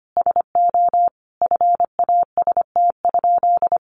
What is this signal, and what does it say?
Morse code, heard as a keyed tone, about 25 wpm, 700 Hz. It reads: SO FAST?
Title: Sentences from Top 1300 Words 25wpm